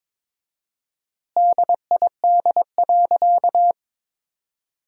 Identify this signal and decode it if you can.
Morse code, heard as a keyed tone, about 21 wpm, 700 Hz. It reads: DID.
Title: 4x Word-Spacing - Sentences from Top 100 Words - 22wpm